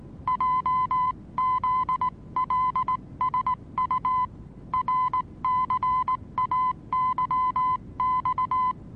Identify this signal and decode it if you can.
Morse code, heard as a keyed tone, about 19 wpm, 1010 Hz. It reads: JZLSU RCAYX
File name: repos / Morsekurs / DJ4UF Morsen26